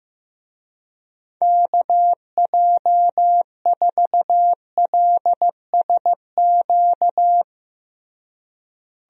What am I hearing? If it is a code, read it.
Morse code, heard as a keyed tone, about 15 wpm, 700 Hz. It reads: KJ4LSQ
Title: USA Call Signs - 6 Characters - Repeated 2x - 15wpm